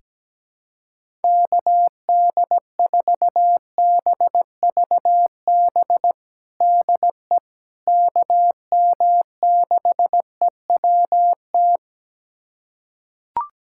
Morse code, heard as a keyed tone, about 17 wpm, 700 Hz. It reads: KD4BVB DE KM6EWT
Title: USA Callsign DE Callsign - 6 Characters - 17wpm